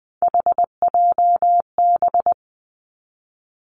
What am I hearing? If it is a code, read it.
Morse code, heard as a keyed tone, about 20 wpm, 700 Hz. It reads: HJB
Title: Three Challenging Letter-Number Combinations 20wpm